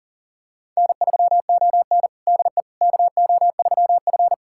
Morse code, heard as a keyed tone, about 38 wpm, 700 Hz. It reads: N3ON DE KO3F
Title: USA Callsign DE Callsign - 4 Characters - 40wpm